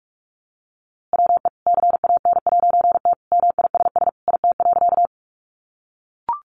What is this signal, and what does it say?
Morse code, heard as a keyed tone, about 44 wpm, 700 Hz. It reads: WE CAN'T MISS IT.
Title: A Sound Of Thunder 45wpm